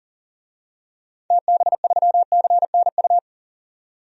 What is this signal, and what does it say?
Morse code, heard as a keyed tone, about 40 wpm, 700 Hz. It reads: TB3CNU
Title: DXCC-100 Call Signs - 6 Characters - 40wpm